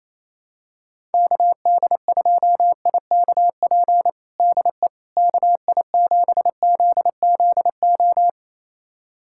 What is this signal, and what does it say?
Morse code, heard as a keyed tone, about 28 wpm, 700 Hz. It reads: KD2IKP DE KI7ZZO
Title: USA Callsign DE Callsign - 6 Characters - 28wpm